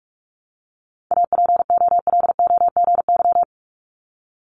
Morse code, heard as a keyed tone, about 46 wpm, 700 Hz. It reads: APOLOGY
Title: Multi-letter Phonemes 45wpm